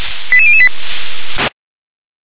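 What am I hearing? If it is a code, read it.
Five-tone selective call (ZVEI), heard as quick runs of five tones, tones of about 70 ms. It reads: 88008
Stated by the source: Received on SDR as a ZVEI five-tone selective call.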